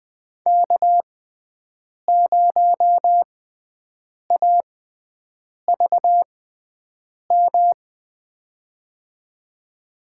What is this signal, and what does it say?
Morse code, heard as a keyed tone, about 20 wpm, spaced as for 7 wpm, 700 Hz. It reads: K0AVM
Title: USA Call Signs - 5 Characters - 6x Character Spacing - 20wpm